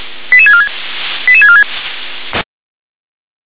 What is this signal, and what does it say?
Five-tone selective call (ZVEI), heard as quick runs of five tones, tones of about 70 ms. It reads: 88646 88646
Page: SDR/ZVEI